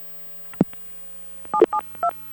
DTMF, heard as telephone touch tones, tones of about 90 ms. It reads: 772